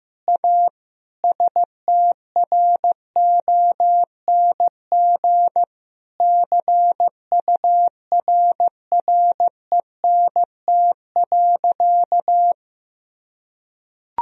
Morse code, heard as a keyed tone, about 14 wpm, 700 Hz. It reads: A STRONG CURRENT.
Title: Sentences from Top 1000 Words - Review Part 4 - 15wpm